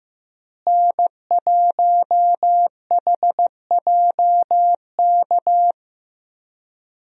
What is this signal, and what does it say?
Morse code, heard as a keyed tone, about 15 wpm, 700 Hz. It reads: N1HJK
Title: USA Call Signs - 5 Characters - Repeated 2x - 15wpm